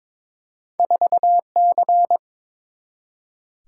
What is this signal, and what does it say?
Morse code, heard as a keyed tone, about 22 wpm, 700 Hz. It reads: 4C
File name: practice / 22wpm / Two Letter-Number Combinations - 22wpm